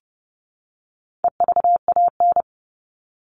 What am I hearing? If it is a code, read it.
Morse code, heard as a keyed tone, about 28 wpm, 700 Hz. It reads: EVAN
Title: All Cut Numbers - 4-Digit 30wpm